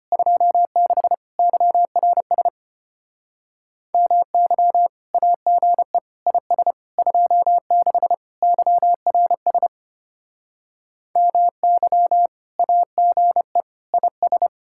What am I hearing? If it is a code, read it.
Morse code, heard as a keyed tone, about 30 wpm, 700 Hz. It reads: 26 YRS MY AGE IS 26 YRS MY AGE IS
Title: Speed-Racing Common QSO Phrases - 25wpm